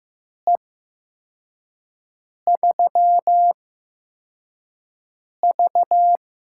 Morse code, heard as a keyed tone, about 15 wpm, spaced as for 4 wpm, 700 Hz. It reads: E3V
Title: USA Call Signs - 4 Characters - 8x Character Spacing - 15wpm